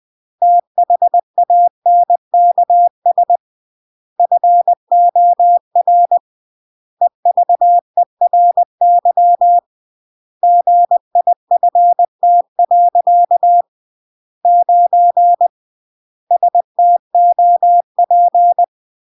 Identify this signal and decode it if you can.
Morse code, heard as a keyed tone, about 20 wpm, 700 Hz. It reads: THANKS FOR EVERY GIFT. 9 STOP
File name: CW / jodorowsky / 82 commandments 20-20